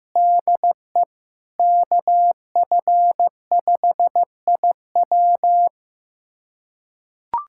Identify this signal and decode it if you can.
Morse code, heard as a keyed tone, about 15 wpm, 700 Hz. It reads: DE KF5IW
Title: USA Callsign DE Callsign - 5 Characters - 15wpm